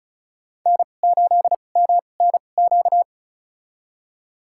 Morse code, heard as a keyed tone, about 34 wpm, spaced as for 25 wpm, 700 Hz. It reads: N8MNQ
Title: USA Call Signs - 5 Characters - 2x Character Spacing - 35wpm